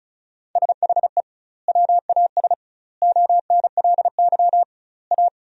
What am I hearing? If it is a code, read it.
Morse code, heard as a keyed tone, about 36 wpm, 700 Hz. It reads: SHE WAS ONLY A
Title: 2x Word-Spacing - Sentences from Top 400 Words - 35wpm